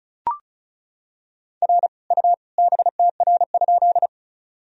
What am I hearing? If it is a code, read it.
Morse code, heard as a keyed tone, about 34 wpm, 700 Hz. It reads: R U BTR?
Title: Sentences from Top 400 Words and Abbreviations 35wpm